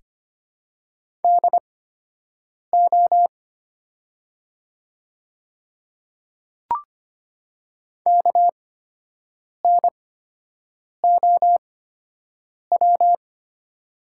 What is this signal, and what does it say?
Morse code, heard as a keyed tone, about 25 wpm, spaced as for 7 wpm, 700 Hz. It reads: DO KNOW
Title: 8x Character Spacing - Top 100 Words - 25wpm